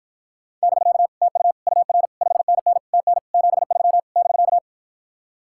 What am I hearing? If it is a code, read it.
Morse code, heard as a keyed tone, about 54 wpm, 700 Hz. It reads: <BK> TU UR 5NN TN 73 <BK>